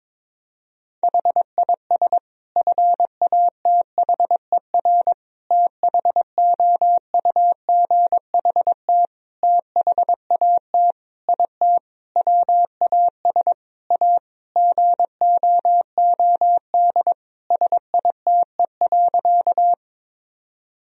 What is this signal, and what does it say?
Morse code, heard as a keyed tone, about 22 wpm, 700 Hz. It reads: HIS FATHER THOUGHT THAT IT WAS A GOOD SITE.
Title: Sentences from Top 1000 Words - Review Part 4 - 22wpm